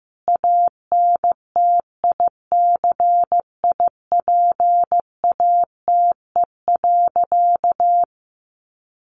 Morse code, heard as a keyed tone, about 15 wpm, 700 Hz. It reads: ANTICIPATE.
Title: Sentences from Top 1500 Words 15wpm